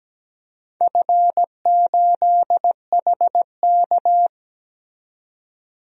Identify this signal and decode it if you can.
Morse code, heard as a keyed tone, about 17 wpm, 700 Hz. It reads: F8HK
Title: DXCC-100 Call Signs - 4 Characters - 17wpm